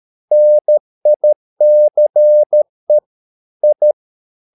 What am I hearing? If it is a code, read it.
Morse code, heard as a keyed tone, about 13 wpm, 600 Hz. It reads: NICE I
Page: Ferie 13 wpm